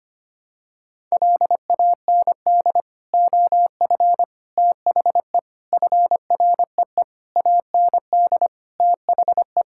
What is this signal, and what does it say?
Morse code, heard as a keyed tone, about 25 wpm, 700 Hz. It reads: LAND OF THE FREE AND THE
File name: Sets of 9 Words - 25wpm